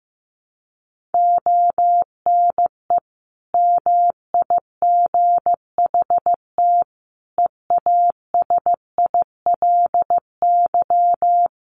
Morse code, heard as a keyed tone, about 15 wpm, 700 Hz. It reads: ONE MIGHT EASILY
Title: Sentences from Top 1400 Words 15wpm